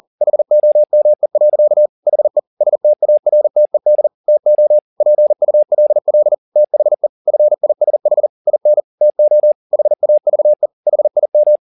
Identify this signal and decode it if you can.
Morse code, heard as a keyed tone, about 40 wpm, 600 Hz. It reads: HOME. HE STARTED TO PULL THE FISH IN TO HAVE HIM
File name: the old man and the sea 2 40wpm